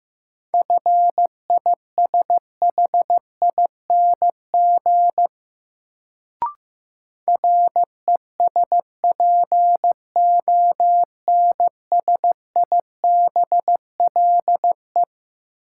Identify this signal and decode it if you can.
Morse code, heard as a keyed tone, about 15 wpm, 700 Hz. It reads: FISHING RESPONSIBLE
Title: Sentences from Top 1600 Words - New Word Drill Alt - 15wpm